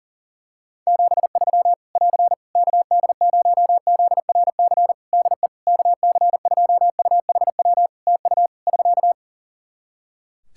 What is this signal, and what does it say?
Morse code, heard as a keyed tone, about 40 wpm, 700 Hz. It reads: 73 <AR> KD0ZRC DE KC2UHW TU <SK>
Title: Practice QSO - Exchange 4 - Ending - 40wpm